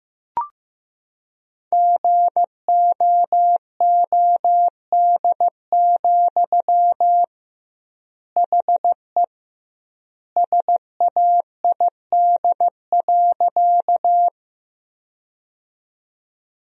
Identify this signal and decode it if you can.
Morse code, heard as a keyed tone, about 15 wpm, 700 Hz. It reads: GOOD, HE SAID.
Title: 2x Word-Spacing - Sentences from Top 200 Words - 15wpm